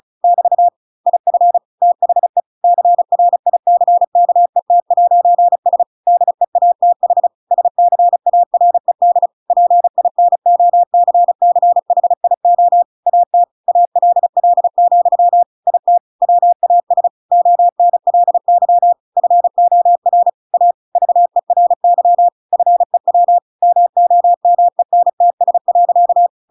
Morse code, heard as a keyed tone, about 34 wpm, 700 Hz. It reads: <BT> IF THE CRICKET'S DEATH SCARED PINOCCHIO AT ALL, IT WAS ONLY FOR A VERY FEW MOMENTS.
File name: Pinocchio / Pinocchio ch5 35 35 ik1hge